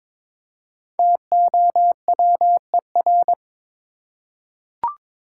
Morse code, heard as a keyed tone, about 23 wpm, 700 Hz. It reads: TOWER
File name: practice / 22wpm / 5 Letter Words - First 3 Letters Given - 22wpm